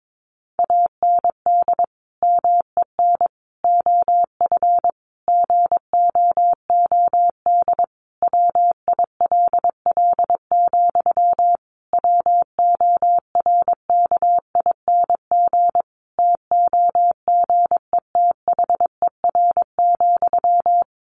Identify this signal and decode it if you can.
Morse code, heard as a keyed tone, about 22 wpm, 700 Hz. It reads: AND MEN OF GOOD WILL, WORKING TOGETHER,
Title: Sentences from Top 1400 Words 22wpm